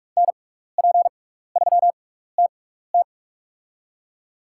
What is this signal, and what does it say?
Morse code, heard as a keyed tone, about 48 wpm, spaced as for 16 wpm, 700 Hz. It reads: NP3TT
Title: USA Call Signs - 5 Characters - 6x Character Spacing - 45wpm